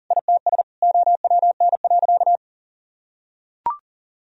Morse code, heard as a keyed tone, about 40 wpm, 700 Hz. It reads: ITS OWN.
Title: Sentences from Top 1000 Words - Review Part 4 - 40wpm